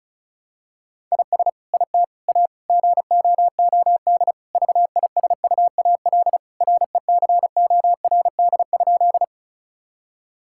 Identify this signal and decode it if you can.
Morse code, heard as a keyed tone, about 36 wpm, 700 Hz. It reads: IS IT A GOOD VISUAL RECORD?
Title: Sentences from Top 1000 Words - Review Part 6 - 35wpm